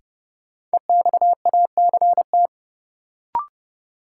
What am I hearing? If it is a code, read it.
Morse code, heard as a keyed tone, about 30 wpm, 700 Hz. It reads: EXACT
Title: Sentences from Top 1300 Words - New Word Drill Alt - 30wpm